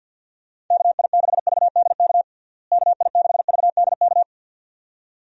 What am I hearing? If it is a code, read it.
Morse code, heard as a keyed tone, about 50 wpm, 700 Hz. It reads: KI6VDK KI6VDK
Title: USA Call Signs - 6 Characters - Repeated 2x - 50wpm